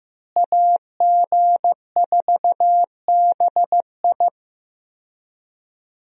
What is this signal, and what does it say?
Morse code, heard as a keyed tone, about 15 wpm, 700 Hz. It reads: AG4BI
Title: USA Call Signs - 5 Characters - Repeated 3x - 15wpm